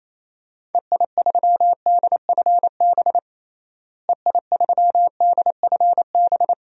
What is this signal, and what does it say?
Morse code, heard as a keyed tone, about 28 wpm, 700 Hz. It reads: EI3DFB EI3DFB
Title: DXCC-100 Call Signs - 6 Characters - Repeated 3x - 28wpm